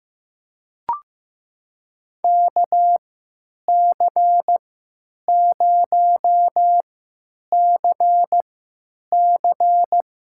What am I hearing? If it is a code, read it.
Morse code, heard as a keyed tone, about 15 wpm, spaced as for 9 wpm, 700 Hz. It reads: KC0CC
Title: USA Call Signs - 3x Character Spacing - 15wpm